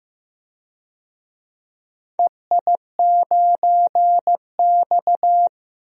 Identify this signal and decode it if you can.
Morse code, heard as a keyed tone, about 15 wpm, 700 Hz. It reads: EI9X
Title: DXCC-100 Call Signs - 4 Characters - 15wpm